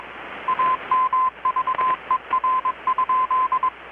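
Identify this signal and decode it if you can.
Morse code, heard as a keyed tone, about 22 wpm, 1030 Hz, with radio noise. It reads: AMVER?
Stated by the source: Morse from KPH's Night of Nights XX maritime radiotelegraphy event, received in Chiapas, Mexico about 3,800 km away on a Kenwood R-600 with a grounded 10 m long-wire antenna.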